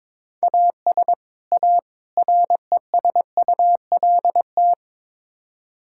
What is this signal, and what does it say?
Morse code, heard as a keyed tone, about 22 wpm, 700 Hz. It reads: AS A RESULT
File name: practice / 22wpm / Build A Phrase - 5-Word Phrases - 22wpm